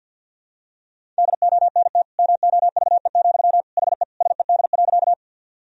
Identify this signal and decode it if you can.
Morse code, heard as a keyed tone, about 52 wpm, 700 Hz. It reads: DONT MOVE, HE SED.
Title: Sentences from Top 300 Words and Abbreviations 50wpm